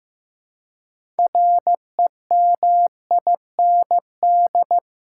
Morse code, heard as a keyed tone, about 15 wpm, 700 Hz. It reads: REMIND
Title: Six Letter Words - 15wpm